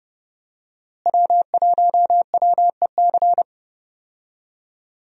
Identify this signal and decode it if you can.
Morse code, heard as a keyed tone, about 31 wpm, 700 Hz. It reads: W1WEC
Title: USA Call Signs - 5 Characters - Repeated 3x - 30wpm